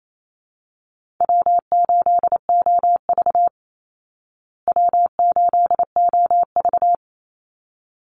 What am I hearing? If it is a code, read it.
Morse code, heard as a keyed tone, about 28 wpm, 700 Hz. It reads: W8OV W8OV
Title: Easy Call Signs Repeated 2x - 28wpm